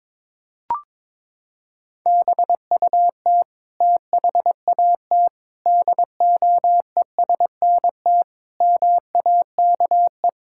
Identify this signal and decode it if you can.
Morse code, heard as a keyed tone, about 22 wpm, 700 Hz. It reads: BUT THAT DOESNT MAKE
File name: practice / 22wpm / Sets of 7 Words - 22wpm